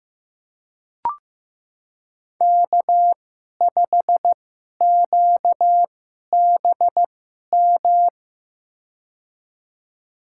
Morse code, heard as a keyed tone, about 15 wpm, spaced as for 11 wpm, 700 Hz. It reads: K5QBM
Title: USA Call Signs - 5 Characters - 2x Character Spacing - 15wpm